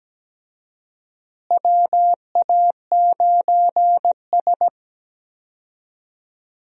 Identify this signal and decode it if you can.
Morse code, heard as a keyed tone, about 17 wpm, 700 Hz. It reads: WA9S
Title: USA Call Signs - 4 Characters - Repeated 3x - 17wpm